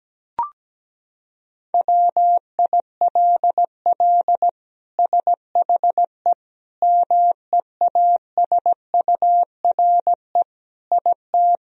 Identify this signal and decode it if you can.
Morse code, heard as a keyed tone, about 17 wpm, 700 Hz. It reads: WILL SHE MEASURE IT
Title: Sentences from Top 1000 Words - Review Part 2 - 17wpm